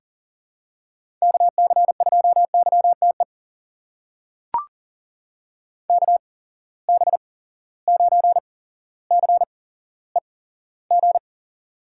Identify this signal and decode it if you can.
Morse code, heard as a keyed tone, about 40 wpm, spaced as for 17 wpm, 700 Hz. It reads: KC2YTE KB9CEG
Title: USA Call Signs - 8x Character Spacing - 40wpm